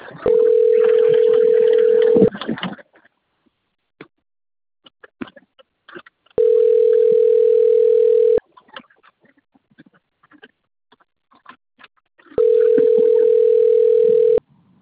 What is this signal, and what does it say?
Ringback tone (North America), 440 + 480 Hz.